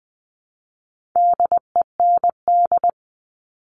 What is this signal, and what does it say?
Morse code, heard as a keyed tone, about 20 wpm, 700 Hz. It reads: DEND
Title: All Cut Numbers - 4-Digit 20wpm